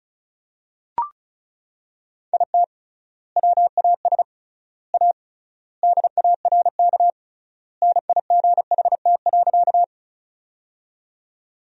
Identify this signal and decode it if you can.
Morse code, heard as a keyed tone, about 34 wpm, 700 Hz. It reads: IT WAS A DARK NIGHT.
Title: 3x Word-Spacing - Sentences from Top 500 Words - 35wpm